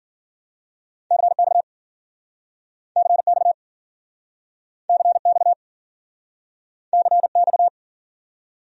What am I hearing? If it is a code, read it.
Morse code, heard as a keyed tone, about 55 wpm, 700 Hz. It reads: CX CX CX CX